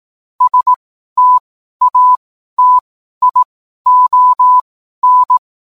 Morse code, heard as a keyed tone, about 18 wpm, spaced as for 12 wpm, 1000 Hz. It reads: STATION
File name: cwblog / cw mp3 directory / mem morse 12